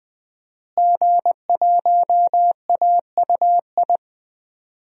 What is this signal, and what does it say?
Morse code, heard as a keyed tone, about 20 wpm, 700 Hz. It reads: G1AUI
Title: DXCC-100 Call Signs - 5 Characters - Repeated 2x - 20wpm